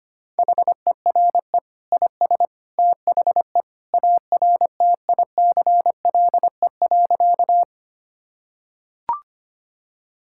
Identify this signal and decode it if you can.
Morse code, heard as a keyed tone, about 25 wpm, 700 Hz. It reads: HERE IS THE ARTICLE.
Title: Sentences from Top 1000 Words - Review Part 5 - 25wpm